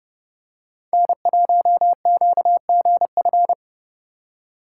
Morse code, heard as a keyed tone, about 31 wpm, 700 Hz. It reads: N1QGF